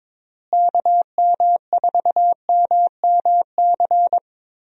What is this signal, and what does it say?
Morse code, heard as a keyed tone, about 22 wpm, 700 Hz. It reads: KM4MMC
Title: USA Call Signs - 6 Characters - Repeated 2x - 22wpm